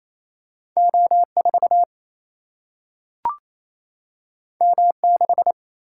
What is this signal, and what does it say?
Morse code, heard as a keyed tone, about 28 wpm, 700 Hz. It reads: O 4 M 6
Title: Single-Letter Single-Number 28wpm